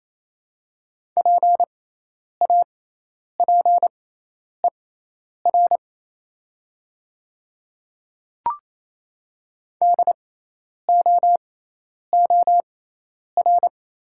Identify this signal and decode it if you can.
Morse code, heard as a keyed tone, about 28 wpm, spaced as for 10 wpm, 700 Hz. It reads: PAPER DOOR
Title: 6x Character Spacing - Top 300 Words - 28wpm